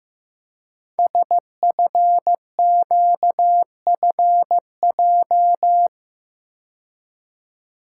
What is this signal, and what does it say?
Morse code, heard as a keyed tone, about 15 wpm, 700 Hz. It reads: SFQFJ
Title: Code Groups - 5 Letters - 15wpm